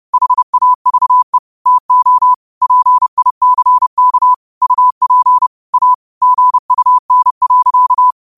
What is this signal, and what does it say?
Morse code, heard as a keyed tone, about 30 wpm, 1000 Hz. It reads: HAVE TO PICK UP A GUN.